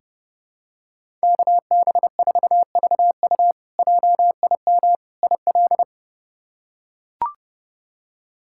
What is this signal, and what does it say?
Morse code, heard as a keyed tone, about 32 wpm, 700 Hz. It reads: KB4VU JIM IL
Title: K1USN SST QSO Practice - 30wpm